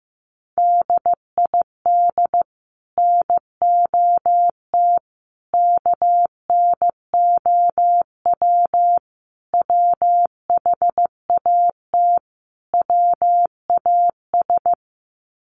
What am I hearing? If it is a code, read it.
Morse code, heard as a keyed tone, about 15 wpm, 700 Hz. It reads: DID NOT KNOW WHAT WAS
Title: Sets of 5 Words Encore2 15wpm